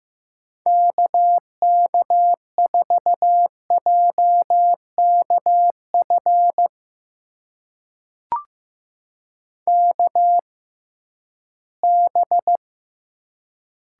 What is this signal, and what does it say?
Morse code, heard as a keyed tone, about 15 wpm, spaced as for 11 wpm, 700 Hz. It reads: KK4JKF KB
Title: USA Call Signs - 6 Characters - 6x Character Spacing - 15wpm